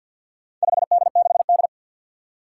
Morse code, heard as a keyed tone, about 52 wpm, 700 Hz. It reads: 5D6D
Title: Four Challenging Letter-Number Combinations 50wpm